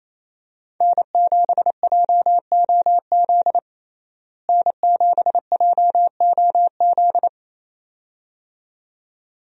Morse code, heard as a keyed tone, about 28 wpm, 700 Hz. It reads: N7JOZ N7JOZ
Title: USA Call Signs - 5 Characters - Repeated 2x - 28wpm